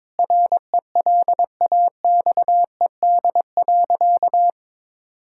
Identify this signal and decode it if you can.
Morse code, heard as a keyed tone, about 22 wpm, 700 Hz. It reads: RELAXED.
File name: practice / 22wpm / Sentences with 4-Word Phrases - 22wpm